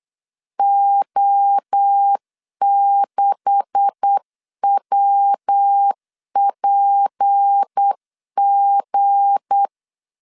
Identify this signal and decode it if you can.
Morse code, heard as a keyed tone, about 8 wpm, 800 Hz. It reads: O6WPG
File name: CW / DET / 1 9 1500C S A 8wpm 5637